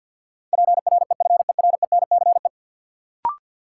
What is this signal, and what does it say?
Morse code, heard as a keyed tone, about 53 wpm, 700 Hz. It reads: PREFERENCE
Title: Sentences from Top 1500 Words - New Word Drill - 50wpm